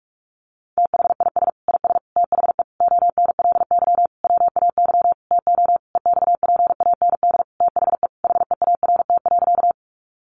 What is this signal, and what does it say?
Morse code, heard as a keyed tone, about 44 wpm, 700 Hz. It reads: THIS IS THE ONLY WAY TO EXPAND THE HEART.
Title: Sentences from Top 1400 Words 45wpm